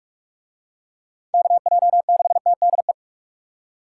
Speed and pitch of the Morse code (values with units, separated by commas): 41 wpm, 700 Hz